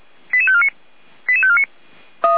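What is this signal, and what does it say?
Five-tone selective call (ZVEI), heard as quick runs of five tones, tones of about 70 ms. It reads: 88549 88549
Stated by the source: ZVEI five-tone selective call 88190, received by SDR.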